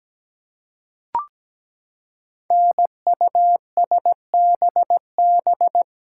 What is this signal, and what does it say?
Morse code, heard as a keyed tone, about 17 wpm, 700 Hz. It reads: NUSBB
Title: Code Groups - 5 Letters - 17wpm